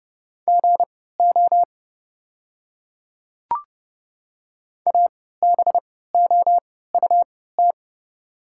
Morse code, heard as a keyed tone, about 30 wpm, spaced as for 17 wpm, 690 Hz. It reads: GO ABOUT
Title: 3x Character Spacing - Top 100 Words - 30wpm